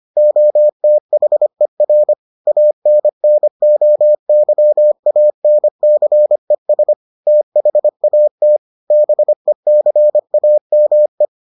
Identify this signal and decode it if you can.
Morse code, heard as a keyed tone, about 25 wpm, 600 Hz. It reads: OTHER ANNOYANCES THAT BECAME